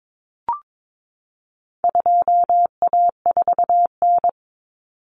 Morse code, heard as a keyed tone, about 22 wpm, 700 Hz. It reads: 2A4N
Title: Common Cut Numbers - 4-Digit 22wpm